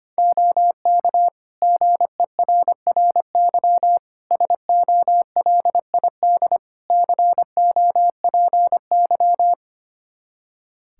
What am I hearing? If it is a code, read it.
Morse code, heard as a keyed tone, about 25 wpm, 700 Hz. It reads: OK GERRY SOLID COPY
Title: Practice QSO - Exchange 3 - More Info - 25wpm